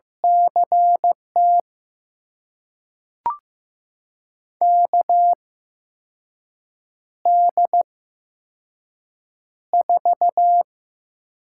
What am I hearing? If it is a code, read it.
Morse code, heard as a keyed tone, about 15 wpm, spaced as for 5 wpm, 700 Hz. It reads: CT KD4
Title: USA Call Signs - 6 Characters - 8x Character Spacing - 15wpm